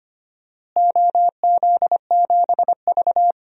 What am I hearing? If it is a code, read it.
Morse code, heard as a keyed tone, about 25 wpm, 700 Hz. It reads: OZ7V
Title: DXCC-100 Call Signs - 4 Characters - Repeated 3x - 25wpm